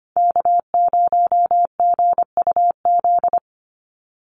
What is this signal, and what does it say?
Morse code, heard as a keyed tone, about 25 wpm, 700 Hz. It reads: K0GUZ
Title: Call Signs Repeated 3x - 25wpm